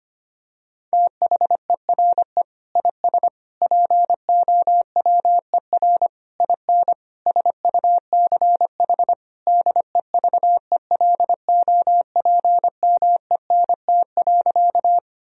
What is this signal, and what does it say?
Morse code, heard as a keyed tone, about 25 wpm, 700 Hz. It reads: THERE IS POWER IN SUCH DEVELOPMENT.
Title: Sentences from Top 1000 Words - Review Part 4 - 25wpm